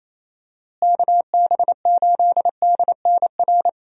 Morse code, heard as a keyed tone, about 28 wpm, 700 Hz. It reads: KB8DNR
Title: USA Call Signs - 6 Characters - Repeated 2x - 28wpm